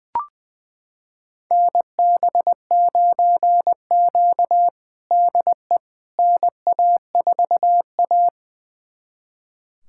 Morse code, heard as a keyed tone, about 20 wpm, 700 Hz. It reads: NB9Q DE NA4A
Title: USA Callsign DE Callsign - 4 Characters - 20wpm